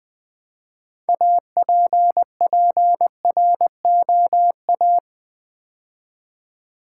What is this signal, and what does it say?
Morse code, heard as a keyed tone, about 20 wpm, 700 Hz. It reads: APPROA
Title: Build A Word - 3 Letters At A Time - 8-Letter Words - 20wpm